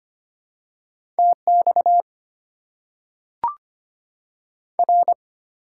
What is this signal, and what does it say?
Morse code, heard as a keyed tone, about 25 wpm, 690 Hz. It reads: TX R
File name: Top 90 QSO Elements - 4x Character Spacing - 25wpm